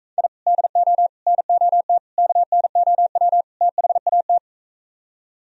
Morse code, heard as a keyed tone, about 46 wpm, 700 Hz. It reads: I DO NOT KNOW THAT